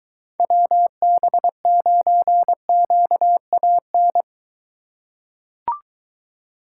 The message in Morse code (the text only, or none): WB9QAN